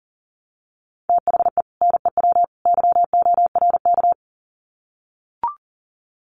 Morse code, heard as a keyed tone, about 40 wpm, 700 Hz. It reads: THE NEW YORK
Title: Sets of 3 Words 40wpm